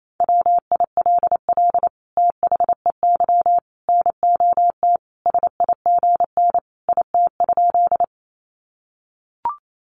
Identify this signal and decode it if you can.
Morse code, heard as a keyed tone, about 28 wpm, 700 Hz. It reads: WILL THEY NOT SIGN IT?
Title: Sentences from Top 1000 Words 28wpm